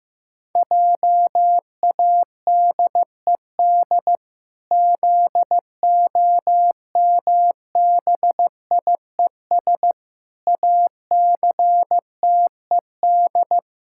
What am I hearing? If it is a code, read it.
Morse code, heard as a keyed tone, about 16 wpm, 700 Hz. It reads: JADED ZOMBIES ACTED
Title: Pangram Challenge v2 - 15wpm